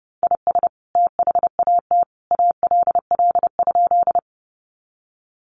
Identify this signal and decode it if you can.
Morse code, heard as a keyed tone, about 30 wpm, 700 Hz. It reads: IS THAT ALL?
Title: Sentences from Top 100 Words 30wpm